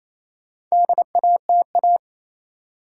Morse code, heard as a keyed tone, about 28 wpm, 700 Hz. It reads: DATA
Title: Sentences from Top 1000 Words - Review Part 3 - New Word Drill - 28wpm